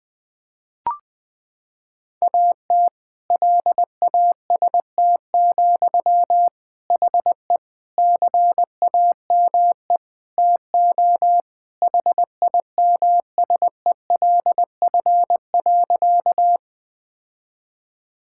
Morse code, heard as a keyed tone, about 20 wpm, 700 Hz. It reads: AT LAST, HE CAME TO HIMSELF.
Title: Sentences from Top 1600 Words 20wpm